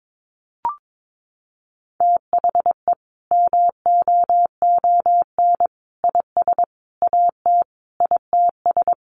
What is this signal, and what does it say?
Morse code, heard as a keyed tone, about 22 wpm, 700 Hz. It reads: THE MOON IS AT ITS
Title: Sentences from Top 1300 Words 22wpm